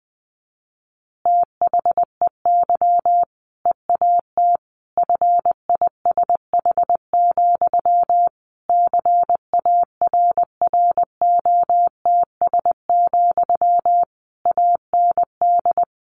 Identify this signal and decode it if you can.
Morse code, heard as a keyed tone, about 19 wpm, 700 Hz. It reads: THEY EAT FISH, CARROTS, AND
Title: Easy Phrases 20wpm